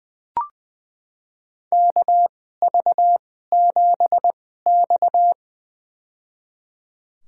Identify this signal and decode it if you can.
Morse code, heard as a keyed tone, about 20 wpm, spaced as for 15 wpm, 700 Hz. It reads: KV7X
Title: USA Call Signs - 4 Characters - 2x Character Spacing - 20wpm